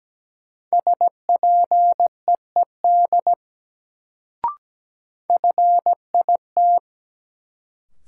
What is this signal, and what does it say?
Morse code, heard as a keyed tone, about 17 wpm, 700 Hz. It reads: SPEED FIT
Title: Sentences from Top 800 Words - New Word Drill - 17wpm